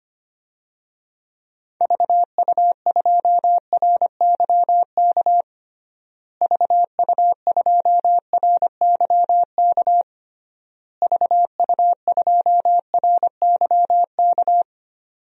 Morse code, heard as a keyed tone, about 25 wpm, 700 Hz. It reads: VU2RYK VU2RYK VU2RYK
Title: DXCC-100 Call Signs - 6 Characters - Repeated 3x - 25wpm